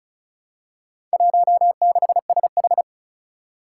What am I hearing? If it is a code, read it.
Morse code, heard as a keyed tone, about 36 wpm, 700 Hz. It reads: 16SH
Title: Four Challenging Letter-Number Combinations 35wpm